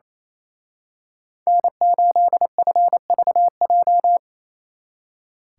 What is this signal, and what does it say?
Morse code, heard as a keyed tone, about 33 wpm, 700 Hz. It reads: N8FVJ
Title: Speed Racing - USA Call Signs 28wpm